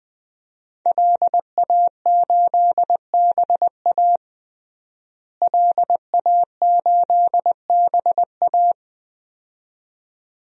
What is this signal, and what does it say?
Morse code, heard as a keyed tone, about 20 wpm, 700 Hz. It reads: LA8BA LA8BA
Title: DXCC-100 Call Signs - 5 Characters - Repeated 2x - 20wpm